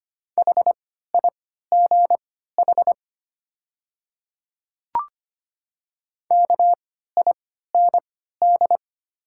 Morse code, heard as a keyed tone, about 25 wpm, spaced as for 14 wpm, 700 Hz. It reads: HIGH KIND
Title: 3x Character Spacing - Top 200 Words - 25wpm